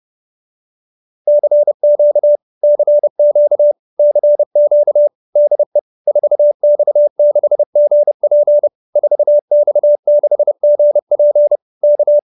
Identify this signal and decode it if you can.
Morse code, heard as a keyed tone, about 30 wpm, 590 Hz. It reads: CQ CQ CQ DE 4X6GP 4X6GP K